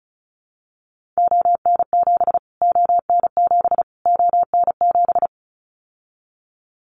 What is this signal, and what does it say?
Morse code, heard as a keyed tone, about 34 wpm, 700 Hz. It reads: ON7 ON7 ON7
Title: Callsign Prefixes for Top 25 DXCC - Repeated 3x - 35wpm